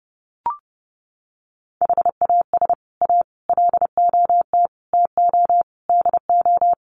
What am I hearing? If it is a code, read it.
Morse code, heard as a keyed tone, about 30 wpm, 700 Hz. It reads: HAS A LOT TO DO
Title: Sets of 5 Words Encore2 30wpm